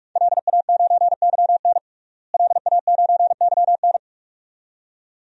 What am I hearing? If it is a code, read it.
Morse code, heard as a keyed tone, about 43 wpm, 700 Hz. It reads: LA9YN LA9YN
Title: DXCC-100 Call Signs - 5 Characters - Repeated 2x - 45wpm